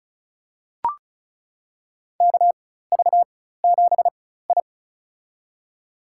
Morse code, heard as a keyed tone, about 34 wpm, spaced as for 16 wpm, 700 Hz. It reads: KV7I
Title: USA Call Signs - 4 Characters - 4x Character Spacing - 35wpm